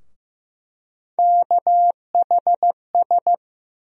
Morse code, heard as a keyed tone, about 15 wpm, 700 Hz. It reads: KHS
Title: USA Call Sign Postfixes - 3 Letters 15wpm